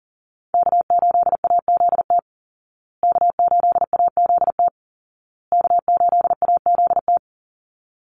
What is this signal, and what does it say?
Morse code, heard as a keyed tone, about 40 wpm, 700 Hz. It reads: K8AZT K8AZT K8AZT
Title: Call Signs Repeated 2x - 40wpm